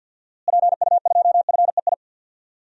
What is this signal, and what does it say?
Morse code, heard as a keyed tone, about 46 wpm, 700 Hz. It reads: PU2FEI
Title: DXCC-100 Call Signs - 6 Characters - Repeated 3x - 50wpm